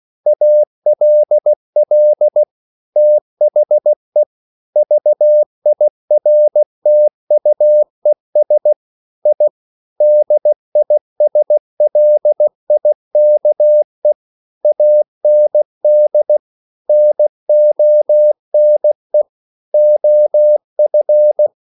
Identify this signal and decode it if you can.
Morse code, heard as a keyed tone, about 16 wpm, 600 Hz. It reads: ALL THE VIRTUES I DISLIKE AND NONE OF
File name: CW / Practice / EnglishQuotations0416